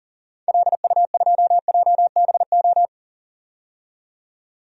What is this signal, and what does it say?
Morse code, heard as a keyed tone, about 40 wpm, 700 Hz. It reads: LU2JBO
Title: DXCC-100 Call Signs - 6 Characters - 40wpm